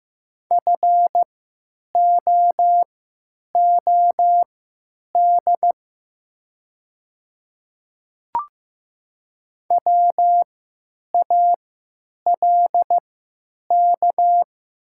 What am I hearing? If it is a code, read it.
Morse code, heard as a keyed tone, about 15 wpm, spaced as for 9 wpm, 700 Hz. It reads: FOOD WALK
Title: 3x Character Spacing - Top 300 Words - 15wpm